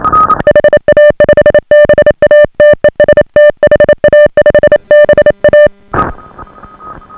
Morse code, heard as a keyed tone, about 27 wpm, 590 Hz. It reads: HA5BA TEST HA5BA